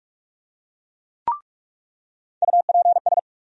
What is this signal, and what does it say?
Morse code, heard as a keyed tone, about 46 wpm, 700 Hz. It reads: UPS
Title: USA Call Sign Postfixes - 3 Letters 45wpm